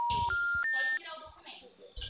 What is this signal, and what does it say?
Special information tone, with speech.